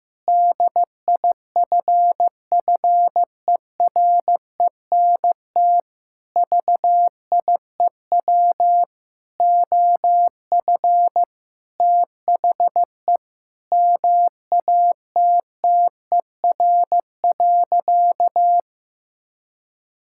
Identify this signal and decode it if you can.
Morse code, heard as a keyed tone, about 16 wpm, 700 Hz. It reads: DIFFERENT VIEW OF THE MATTER.
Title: Sentences from Top 1600 Words 15wpm